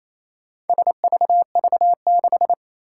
Morse code, heard as a keyed tone, about 29 wpm, 700 Hz. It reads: SVV6
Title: Four Challenging Letter-Number Combinations 28wpm